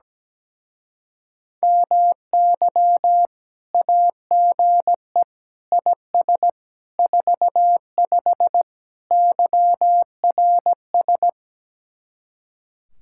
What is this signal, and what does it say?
Morse code, heard as a keyed tone, about 17 wpm, 700 Hz. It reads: MY AGE IS 45 YRS
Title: Speed-Racing Common QSO Phrases - 17wpm